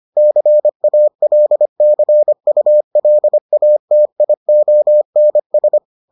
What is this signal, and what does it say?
Morse code, heard as a keyed tone, about 25 wpm, 600 Hz. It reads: CALCULATIONS